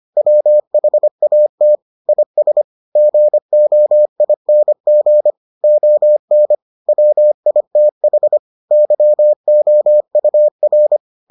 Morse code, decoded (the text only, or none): WHAT IS GOING ON WITH YOUR